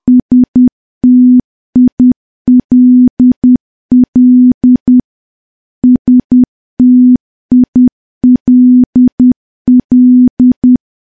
Morse code, heard as a keyed tone, about 10 wpm, 250 Hz. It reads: STILL STILL
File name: morse / morse-training-10WPM